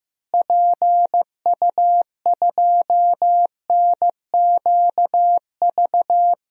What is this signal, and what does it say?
Morse code, heard as a keyed tone, about 15 wpm, 700 Hz. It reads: PU2NQV
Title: DXCC-100 Call Signs - 6 Characters - Repeated 3x - 15wpm